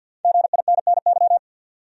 Morse code, heard as a keyed tone, about 53 wpm, 700 Hz. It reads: GINNY
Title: USA Common Names - Max 5 Letters - 50wpm